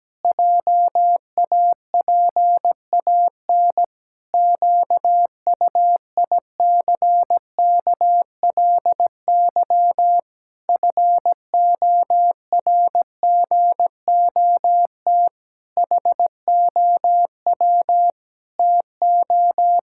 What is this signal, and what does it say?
Morse code, heard as a keyed tone, about 17 wpm, 700 Hz. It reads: JAPAN QUICKLY FORGOT HOW TO
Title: Pangram Challenge v2 - 17wpm